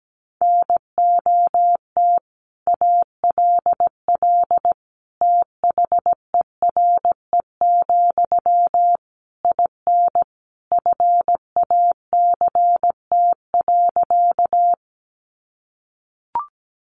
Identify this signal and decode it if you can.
Morse code, heard as a keyed tone, about 17 wpm, 700 Hz. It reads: NOT ALL THERE, IN FACT.
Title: Sentences from Top 600 Words 17wpm